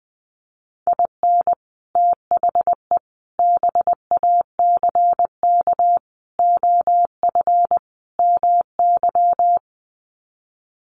Morse code, heard as a keyed tone, about 20 wpm, 700 Hz. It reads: IN THE BACK OF MY
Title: Sets of 5 Words Encore 20wpm